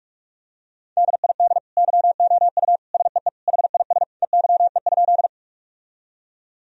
Morse code, heard as a keyed tone, about 42 wpm, 700 Hz. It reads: DID YOU SEE HIS EYE?
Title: Sentences from Top 1000 Words - Review Part 1 - 45wpm